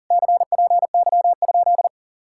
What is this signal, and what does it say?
Morse code, heard as a keyed tone, about 47 wpm, 700 Hz. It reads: CPY?
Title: Speed-Racing Common QSO Phrases - 40wpm